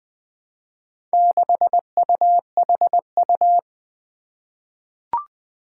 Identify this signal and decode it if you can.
Morse code, heard as a keyed tone, about 21 wpm, 700 Hz. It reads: 6UHU
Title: Four Challenging Letter-Number Combinations 20wpm